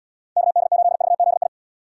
Morse code, heard as a keyed tone, about 78 wpm, 700 Hz. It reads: KD8VZI